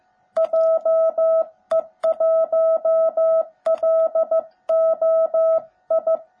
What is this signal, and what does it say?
Morse code, heard as a keyed tone, about 14 wpm, 650 Hz. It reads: JE1LOI
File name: jh8cbh / 2021cw / cw17b